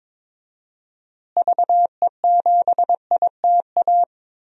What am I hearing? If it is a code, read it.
Morse code, heard as a keyed tone, about 22 wpm, 700 Hz. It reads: VE7ITA
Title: DXCC-100 Call Signs - 6 Characters - Repeated 2x - 22wpm